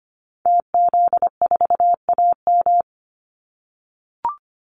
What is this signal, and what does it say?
Morse code, heard as a keyed tone, about 25 wpm, 700 Hz. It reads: TZ4AM
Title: Call Signs - 25wpm